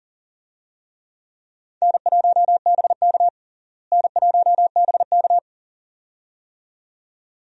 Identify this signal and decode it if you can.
Morse code, heard as a keyed tone, about 40 wpm, 700 Hz. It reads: N1BK N1BK
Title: USA Call Signs - 4 Characters - Repeated 2x - 40wpm